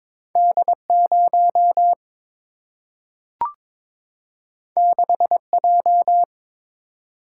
Morse code, heard as a keyed tone, about 22 wpm, 700 Hz. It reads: D 0 6 J
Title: Single-Letter Single-Number 22wpm